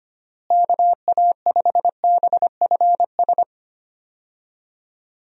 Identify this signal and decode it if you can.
Morse code, heard as a keyed tone, about 25 wpm, 700 Hz. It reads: KA5BFS
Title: USA Call Signs - 6 Characters - Repeated 2x - 25wpm